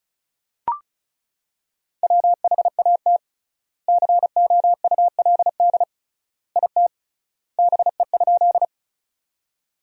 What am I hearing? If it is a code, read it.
Morse code, heard as a keyed tone, about 36 wpm, 700 Hz. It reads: WHAT COULD IT BE?
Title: 3x Word-Spacing - Sentences from Top 100 Words - 35wpm